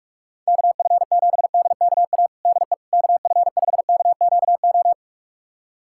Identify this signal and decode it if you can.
Morse code, heard as a keyed tone, about 41 wpm, 700 Hz. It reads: KF7DKA DE KF5KQO
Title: USA Callsign DE Callsign - 6 Characters - 45wpm